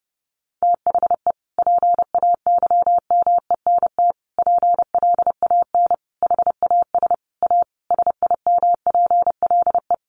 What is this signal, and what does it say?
Morse code, heard as a keyed tone, about 30 wpm, 700 Hz. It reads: THE PAYMENT PLAN HAS A SIMPLE
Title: Sentences from Top 1500 Words 30wpm